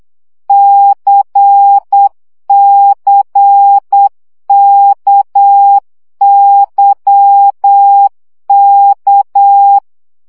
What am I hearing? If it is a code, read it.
Morse code, heard as a keyed tone, about 8 wpm, 800 Hz. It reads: CCKYK
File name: CW gyakorlas k c y 7 baud